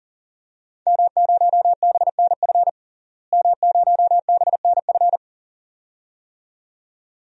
Morse code, heard as a keyed tone, about 40 wpm, 700 Hz. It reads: M0BNF M0BNF
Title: DXCC-100 Call Signs - 5 Characters - Repeated 2x - 40wpm